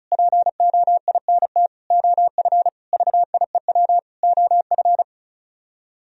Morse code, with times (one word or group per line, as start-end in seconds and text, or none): 0.1-1.7 POINT
1.9-2.7 OF
2.9-4.0 VIEW
4.2-5.0 OF